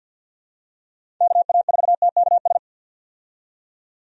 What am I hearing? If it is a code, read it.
Morse code, heard as a keyed tone, about 47 wpm, 700 Hz. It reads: KA4TKS
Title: USA Call Signs - 6 Characters - 50wpm